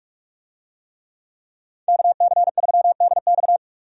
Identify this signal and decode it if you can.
Morse code, heard as a keyed tone, about 45 wpm, 700 Hz. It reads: KC3DX
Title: USA Call Signs - 5 Characters - 45wpm